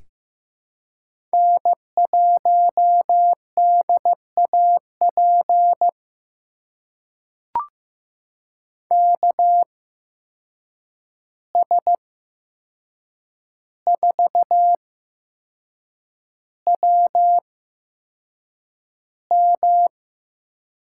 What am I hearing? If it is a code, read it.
Morse code, heard as a keyed tone, about 15 wpm, spaced as for 6 wpm, 700 Hz. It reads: N1DAP KS4WM
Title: USA Call Signs - 5 Characters - 8x Character Spacing - 15wpm